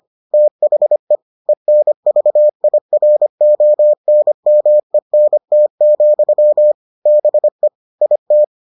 Morse code, heard as a keyed tone, about 25 wpm, 600 Hz. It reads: THE ENVIRONMENT, BE IT